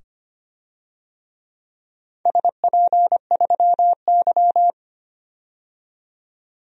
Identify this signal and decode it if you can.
Morse code, heard as a keyed tone, about 25 wpm, 700 Hz. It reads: SP3Y
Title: DXCC-100 Call Signs - 4 Characters - 25wpm